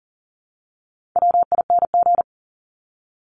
Morse code, heard as a keyed tone, about 40 wpm, 700 Hz. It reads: WING
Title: Top 1001-1250 Words 40wpm